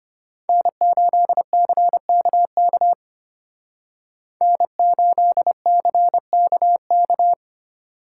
Morse code, heard as a keyed tone, about 33 wpm, 700 Hz. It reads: N8CKK N8CKK